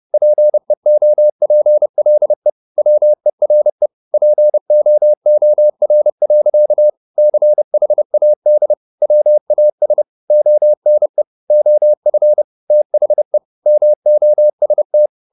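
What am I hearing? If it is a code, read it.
Morse code, heard as a keyed tone, about 30 wpm, 600 Hz. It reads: PEOPLE WERE POOR. CHAD WAS ONE OF THE MOST
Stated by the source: Morse code at 30 wpm, the novel Never by Ken Follett in English, from CW med Gnister.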